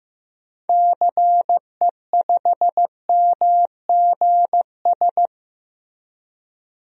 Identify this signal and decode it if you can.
Morse code, heard as a keyed tone, about 15 wpm, 700 Hz. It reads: CE5MGS
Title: DXCC-100 Call Signs - 6 Characters - Repeated 3x - 15wpm